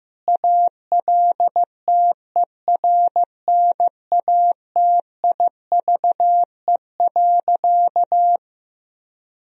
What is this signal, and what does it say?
Morse code, heard as a keyed tone, about 16 wpm, 700 Hz. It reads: ALTERNATIVE.
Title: Sentences from Top 1000 Words - Review Part 4 - 15wpm